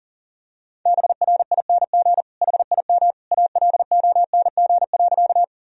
Morse code, heard as a keyed tone, about 40 wpm, 700 Hz. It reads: BRING HIM ALONG.
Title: Sentences from Top 1000 Words - Review Part 3 - 40wpm